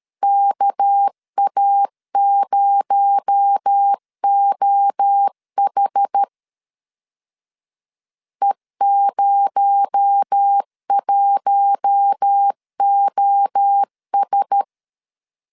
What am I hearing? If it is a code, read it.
Morse code, heard as a keyed tone, about 13 wpm, 800 Hz. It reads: KA0OH E01OS
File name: CW / DET / 1 6 1500C S B 12wpm 3431